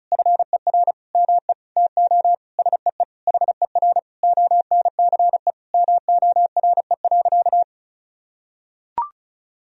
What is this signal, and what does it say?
Morse code, heard as a keyed tone, about 34 wpm, 700 Hz. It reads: FER ME TO SEE HER ONCE MORE.
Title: Sentences from Top 500 Words and Abbreviations 35wpm